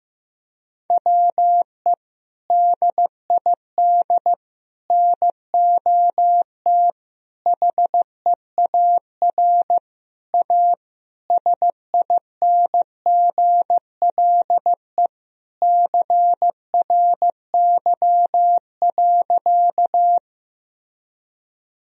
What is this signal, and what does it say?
Morse code, heard as a keyed tone, about 15 wpm, 700 Hz. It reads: WE DID NOT HEAR A SINGLE CRY.
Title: Sentences from Top 1000 Words - Review Part 4 - 15wpm